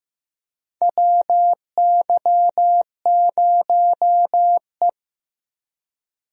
Morse code, heard as a keyed tone, about 16 wpm, 700 Hz. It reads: WY0E
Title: USA Call Signs - 4 Characters - Repeated 3x - 15wpm